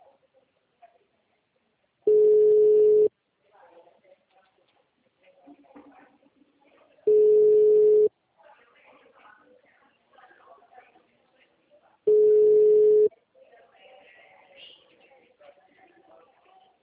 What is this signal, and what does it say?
Ringback tone (Europe, 425 Hz), with speech.